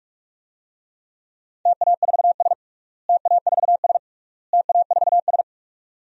Morse code, heard as a keyed tone, about 41 wpm, 700 Hz. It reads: TA4S TA4S TA4S
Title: DXCC-100 Call Signs - 4 Characters - Repeated 3x - 45wpm